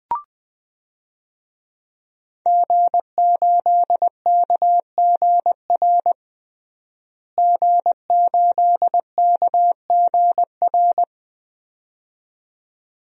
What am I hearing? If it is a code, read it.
Morse code, heard as a keyed tone, about 20 wpm, 700 Hz. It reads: G8KGR G8KGR
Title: DXCC-100 Call Signs - 5 Characters - Repeated 2x - 20wpm